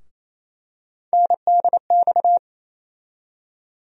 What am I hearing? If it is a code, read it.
Morse code, heard as a keyed tone, about 28 wpm, 700 Hz. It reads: NDX
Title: USA Call Sign Postfixes - 3 Letters 28wpm